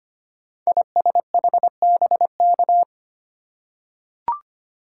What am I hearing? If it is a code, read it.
Morse code, heard as a keyed tone, about 25 wpm, 700 Hz. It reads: ISHBK
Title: Code Groups - 5 Letters - 25wpm